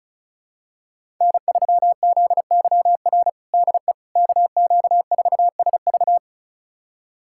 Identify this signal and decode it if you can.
Morse code, heard as a keyed tone, about 36 wpm, 700 Hz. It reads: N3ZYR DE KQ4SV
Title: USA Callsign DE Callsign - 5 Characters - 35wpm